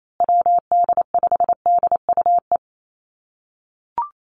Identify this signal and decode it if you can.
Morse code, heard as a keyed tone, about 28 wpm, 700 Hz. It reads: WD5DUE
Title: USA Call Signs Repeated 3x 28wpm